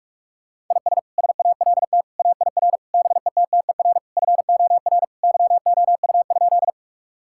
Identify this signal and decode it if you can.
Morse code, heard as a keyed tone, about 47 wpm, 700 Hz. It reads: IS SALT AIR BETTER FOR YOU?
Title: Sentences from Top 1000 Words - Review Part 5 - 45wpm